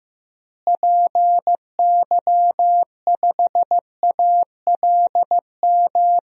Morse code, heard as a keyed tone, about 15 wpm, 700 Hz. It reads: PY5ALM
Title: DXCC-100 Call Signs - 6 Characters - Repeated 3x - 15wpm